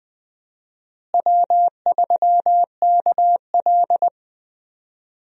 Morse code, heard as a keyed tone, about 21 wpm, 700 Hz. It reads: W3KL
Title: USA Call Signs - 4 Characters - Repeated 3x - 20wpm